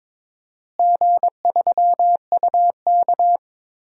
Morse code, heard as a keyed tone, about 22 wpm, 700 Hz. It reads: G3UK
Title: DXCC-100 Call Signs - 4 Characters - Repeated 3x - 22wpm